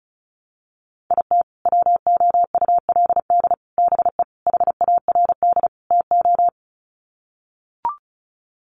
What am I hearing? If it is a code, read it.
Morse code, heard as a keyed tone, about 34 wpm, 700 Hz. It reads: IT WOULD BE HARD TO
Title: Sets of 5 Words Encore2 35wpm